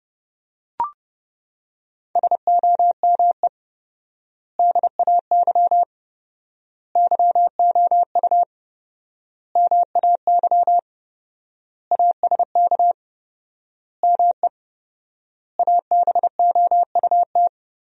Morse code, heard as a keyed tone, about 32 wpm, 700 Hz. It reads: SOME DAY YOU MAY ASK ME ABOUT
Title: 4x Word-Spacing - Sentences from Top 200 Words - 30wpm